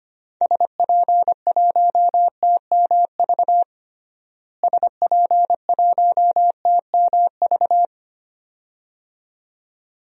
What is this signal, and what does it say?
Morse code, heard as a keyed tone, about 25 wpm, 700 Hz. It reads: SP1TMV SP1TMV
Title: DXCC-100 Call Signs - 6 Characters - Repeated 2x - 25wpm